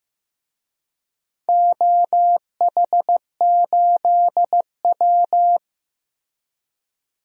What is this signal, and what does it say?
Morse code, heard as a keyed tone, about 15 wpm, 700 Hz. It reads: OH8W
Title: DXCC-100 Call Signs - 4 Characters - Repeated 2x - 15wpm